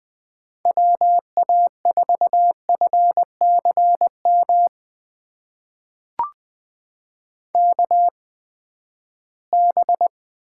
Morse code, heard as a keyed tone, about 20 wpm, spaced as for 14 wpm, 700 Hz. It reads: WA4FCM KB
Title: USA Call Signs - 6 Characters - 8x Character Spacing - 20wpm